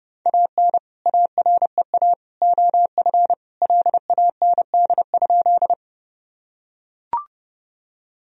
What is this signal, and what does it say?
Morse code, heard as a keyed tone, about 30 wpm, 700 Hz. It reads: AN AREA OF LAND?
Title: Sentences from Top 1000 Words - Review Part 2 - 30wpm